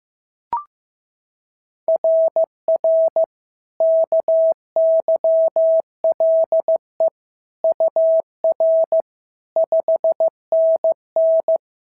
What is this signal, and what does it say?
Morse code, heard as a keyed tone, about 14 wpm, 650 Hz. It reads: RR KYLE UR 5NN
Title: SOTA QSO Practice 15wpm